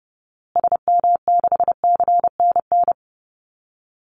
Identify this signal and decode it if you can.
Morse code, heard as a keyed tone, about 30 wpm, 700 Hz. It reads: SM6CNN
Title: Call Signs Repeated 3x - 30wpm